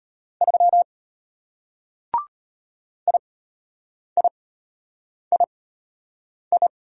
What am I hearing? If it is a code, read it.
Morse code, heard as a keyed tone, about 33 wpm, spaced as for 8 wpm, 700 Hz. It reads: 3 IIII